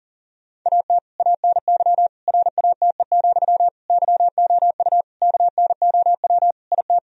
Morse code, heard as a keyed tone, about 40 wpm, 700 Hz. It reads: AT ANY RATE, YOU KNOW IT
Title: Sentences from Top 1000 Words - Review Part 4 - 40wpm